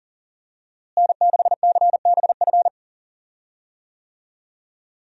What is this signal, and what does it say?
Morse code, heard as a keyed tone, about 40 wpm, 700 Hz. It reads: N6CBF
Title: USA Call Signs - 5 Characters - Repeated 2x - 40wpm